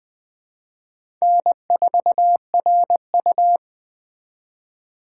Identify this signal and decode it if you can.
Morse code, heard as a keyed tone, about 20 wpm, 700 Hz. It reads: N4RU
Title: Speed Racing - USA 1x2 Call Signs 17wpm